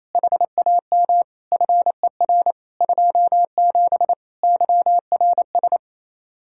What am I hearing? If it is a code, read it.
Morse code, heard as a keyed tone, about 28 wpm, 700 Hz. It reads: HAM FER 27 YRS
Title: Practice QSO - Exchange 3 - More Info - 28wpm